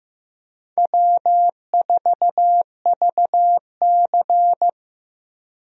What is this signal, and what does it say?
Morse code, heard as a keyed tone, about 15 wpm, 700 Hz. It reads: W4VC
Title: USA Call Signs - 4 Characters - Repeated 3x - 15wpm